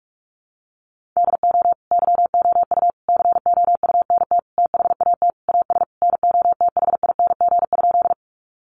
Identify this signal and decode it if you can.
Morse code, heard as a keyed tone, about 44 wpm, 700 Hz. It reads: DO YOU COUNT THAT AS NOTHING?
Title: Sentences from Top 1500 Words 45wpm